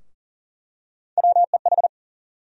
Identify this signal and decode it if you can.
Morse code, heard as a keyed tone, about 40 wpm, 700 Hz. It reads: WEH
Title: USA Call Sign Postfixes - 3 Letters 40wpm